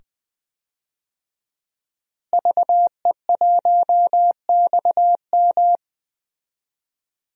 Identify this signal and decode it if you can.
Morse code, heard as a keyed tone, about 20 wpm, 700 Hz. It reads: VE1XM
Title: DXCC-100 Call Signs - 5 Characters - Repeated 2x - 20wpm